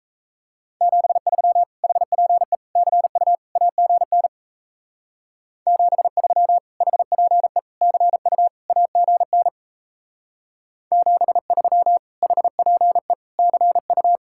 Morse code, heard as a keyed tone, about 32 wpm, 700 Hz. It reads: 73 HPE CU AGN 73 HPE CU AGN 73 HPE CU
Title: Speed-Racing Common QSO Phrases - 28wpm